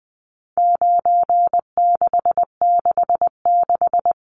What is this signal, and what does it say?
Morse code, heard as a keyed tone, about 20 wpm, 700 Hz. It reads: 9666
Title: Numbers - 4-Digit 20wpm